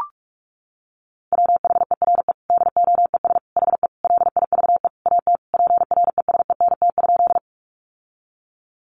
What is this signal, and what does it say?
Morse code, heard as a keyed tone, about 44 wpm, 700 Hz. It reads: WHERE DOES HE LIVE AT PRESENT?
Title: Sentences from Top 800 Words 45wpm